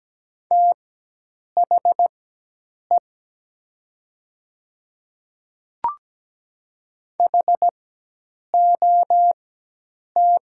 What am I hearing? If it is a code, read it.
Morse code, heard as a keyed tone, about 17 wpm, spaced as for 8 wpm, 700 Hz. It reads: THE HOT
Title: 4x Character Spacing - Top 200 Words - 17wpm